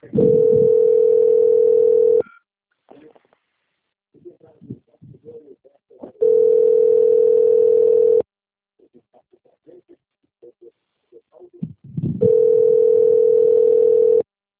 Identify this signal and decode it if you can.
Ringback tone (North America), 440 + 480 Hz, with speech.
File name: out- - - - - .1314